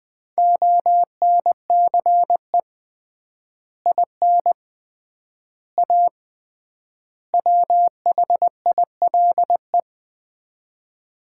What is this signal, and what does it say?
Morse code, heard as a keyed tone, about 20 wpm, 700 Hz. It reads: ONCE IN A WHILE
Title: 3x Word-Spacing - Sets of 4 Words - 20wpm